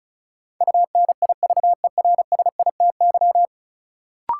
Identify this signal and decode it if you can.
Morse code, heard as a keyed tone, about 36 wpm, 700 Hz. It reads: UNIVERSITY
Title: Sentences from Top 1300 Words - New Word Drill Alt - 35wpm